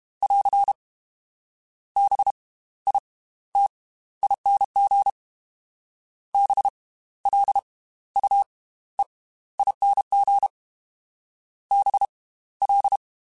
Morse code, heard as a keyed tone, about 32 wpm, spaced as for 16 wpm, 800 Hz. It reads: <AR> BITING BLUEING BL
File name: Slow ing @30